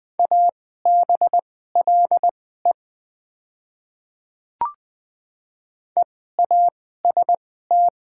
Morse code, heard as a keyed tone, about 21 wpm, spaced as for 15 wpm, 690 Hz. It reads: ABLE EAST
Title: 2x Character Spacing - Top 500 Words - 20wpm